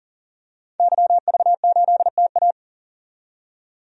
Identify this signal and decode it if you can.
Morse code, heard as a keyed tone, about 40 wpm, 700 Hz. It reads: YV8TA
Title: DXCC-100 Call Signs - 5 Characters - 40wpm